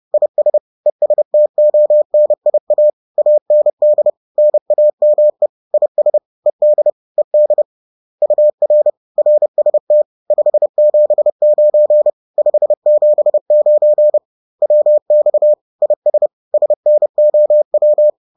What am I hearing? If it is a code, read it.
Morse code, heard as a keyed tone, about 30 wpm, 600 Hz. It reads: IS ESTONIA AND NAME IS ED ED UR RST 579 579 WX IS SNOW